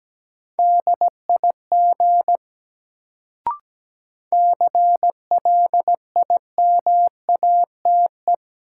Morse code, heard as a keyed tone, about 17 wpm, 700 Hz. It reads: DIG CLIMATE
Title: Sentences from Top 1600 Words - New Word Drill - 17wpm